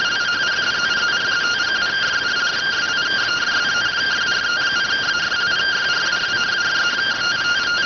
Two-tone teleprinter signal, about 170 Hz shift.